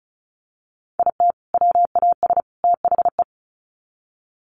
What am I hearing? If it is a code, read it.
Morse code, heard as a keyed tone, about 34 wpm, 700 Hz. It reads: IT WAS THE
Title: Sets of 3 Words 35wpm